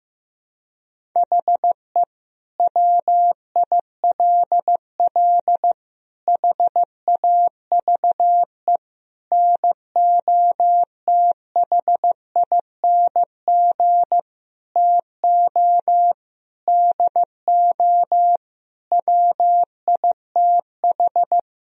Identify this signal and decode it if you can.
Morse code, heard as a keyed tone, about 15 wpm, 700 Hz. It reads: HE WILL HAVE NOTHING TO DO WITH
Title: Sentences from Top 1000 Words - Review Part 4 - 15wpm